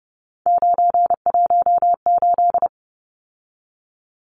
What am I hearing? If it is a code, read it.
Morse code, heard as a keyed tone, about 30 wpm, 700 Hz. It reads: 918
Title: Numbers - 3-Digit 30wpm